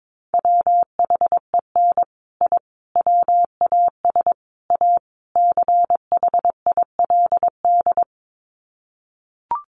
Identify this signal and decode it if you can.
Morse code, heard as a keyed tone, about 22 wpm, 700 Hz. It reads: WHEN I WAS A CHILD
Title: Sets of 5 Words Encore 22wpm